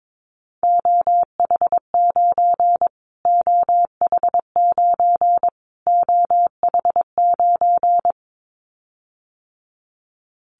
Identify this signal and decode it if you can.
Morse code, heard as a keyed tone, about 22 wpm, 700 Hz. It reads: OH9 OH9 OH9
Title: Callsign Prefixes for Top 25 DXCC - Repeated 3x - 22wpm